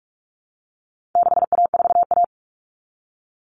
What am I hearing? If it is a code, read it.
Morse code, heard as a keyed tone, about 44 wpm, 700 Hz. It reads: 6A4A
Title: Common Cut Numbers - 4-Digit 45wpm